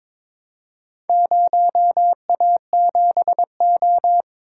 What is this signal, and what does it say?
Morse code, heard as a keyed tone, about 22 wpm, 700 Hz. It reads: 0A7O
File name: practice / 22wpm / Four Letter-Number Combinations - 22wpm